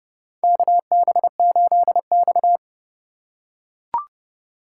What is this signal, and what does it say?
Morse code, heard as a keyed tone, about 30 wpm, 700 Hz. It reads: KB8X
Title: USA Call Signs - 4 Characters - 4x Character Spacing - 30wpm